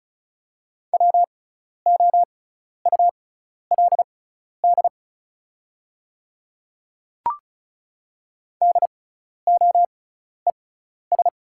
Morse code, heard as a keyed tone, about 34 wpm, spaced as for 12 wpm, 690 Hz. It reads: WOULD DOES
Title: 6x Character Spacing - Top 300 Words - 35wpm